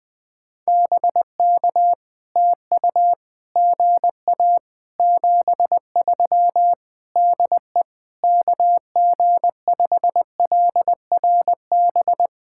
Morse code, heard as a keyed tone, about 20 wpm, 700 Hz. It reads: BK TU GA 73 DE KG5LRB
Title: POTA QSO Practice - 20wpm